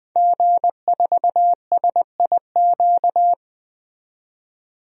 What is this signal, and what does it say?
Morse code, heard as a keyed tone, about 20 wpm, 700 Hz. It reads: G4SIQ
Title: DXCC-100 Call Signs - 5 Characters - 20wpm